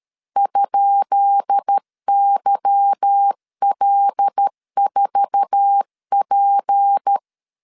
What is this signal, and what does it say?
Morse code, heard as a keyed tone, about 13 wpm, 800 Hz. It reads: ?YL4P
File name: LEZIONI / 10 SIGNS / 10B INT 12wpm 2414